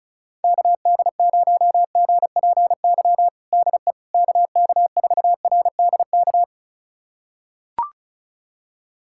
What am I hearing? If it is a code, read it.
Morse code, heard as a keyed tone, about 34 wpm, 700 Hz. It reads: KD0GPY DE KK4RDK
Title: USA Callsign DE Callsign 35wpm